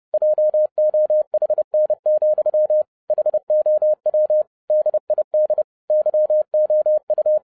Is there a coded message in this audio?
Yes: Morse code from 0.1 to 7.4 s, heard as a keyed tone, about 30 wpm, 610 Hz. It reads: JOHN, HOW DID YOU